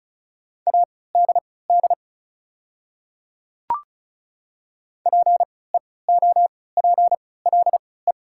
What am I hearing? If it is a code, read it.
Morse code, heard as a keyed tone, about 35 wpm, spaced as for 20 wpm, 700 Hz. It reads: ADD PEOPLE
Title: 3x Character Spacing - Top 200 Words - 35wpm